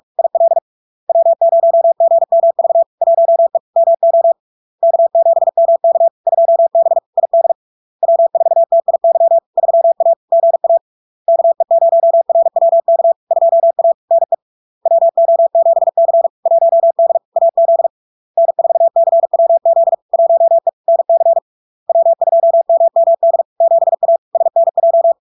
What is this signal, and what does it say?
Morse code, heard as a keyed tone, about 46 wpm, 670 Hz. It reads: IL W0GMV 1E MO K7MK 2B ID W4TIY 3A GA KE0RWK 2A NE WO7C 1D AZ N4CWZ 1E NC W2MMD 7A SNJ
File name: LARK-FD-2022-Log-in-CW-45wpm-670-Hz